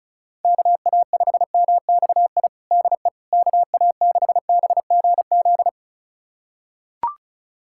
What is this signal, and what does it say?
Morse code, heard as a keyed tone, about 34 wpm, 700 Hz. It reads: KA5MXI DE KA6BGZ
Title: USA Callsign DE Callsign - 6 Characters - 35wpm